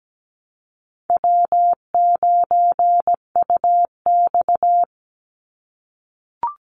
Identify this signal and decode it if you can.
Morse code, heard as a keyed tone, about 17 wpm, 700 Hz. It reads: W9UX
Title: NA SOTA Chaser Callsigns 17wpm